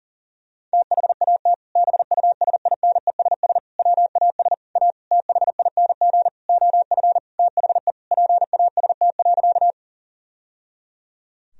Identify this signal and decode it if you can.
Morse code, heard as a keyed tone, about 37 wpm, 700 Hz. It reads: THAT BUSINESS WAS A THING OF THE PAST.
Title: Sentences from Top 1000 Words - Review Part 3 - 40wpm